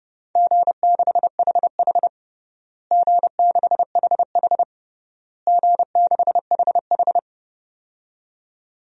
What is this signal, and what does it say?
Morse code, heard as a keyed tone, about 30 wpm, 700 Hz. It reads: G6HH G6HH G6HH
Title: DXCC-100 Call Signs - 4 Characters - Repeated 3x - 30wpm